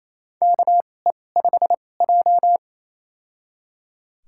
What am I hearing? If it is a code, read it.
Morse code, heard as a keyed tone, about 28 wpm, spaced as for 20 wpm, 700 Hz. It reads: KE5J
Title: USA Call Signs - 4 Characters - 2x Character Spacing - 28wpm